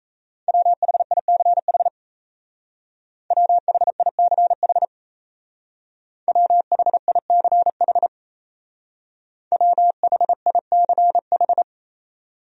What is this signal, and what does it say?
Morse code, heard as a keyed tone, about 33 wpm, 700 Hz. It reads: WHICH WHICH WHICH WHICH
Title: Speed-Racing Top 200 Words 28wpm